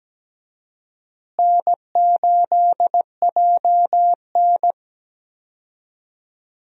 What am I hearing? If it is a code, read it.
Morse code, heard as a keyed tone, about 17 wpm, 700 Hz. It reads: N8JN
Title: USA Call Signs - 4 Characters - Repeated 2x - 17wpm